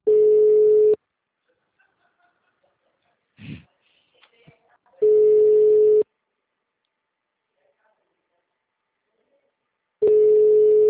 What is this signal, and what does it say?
Ringback tone (Europe, 425 Hz), with speech.